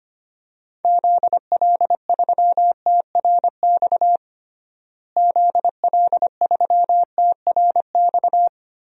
Morse code, heard as a keyed tone, about 25 wpm, 700 Hz. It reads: ZL3TRX ZL3TRX
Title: DXCC-100 Call Signs - 6 Characters - Repeated 3x - 25wpm